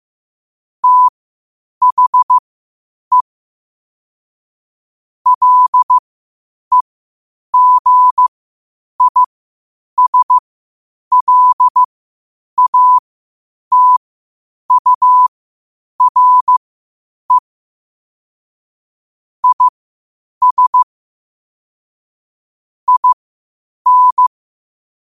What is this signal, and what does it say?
Morse code, heard as a keyed tone, about 15 wpm, spaced as for 8 wpm, 1000 Hz. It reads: THE LEGISLATURE IS IN
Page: New quotes every day in morse code at 8 Words per minute.